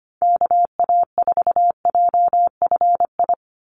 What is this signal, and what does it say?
Morse code, heard as a keyed tone, about 25 wpm, 700 Hz. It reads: KA4JFI
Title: USA Call Signs 25wpm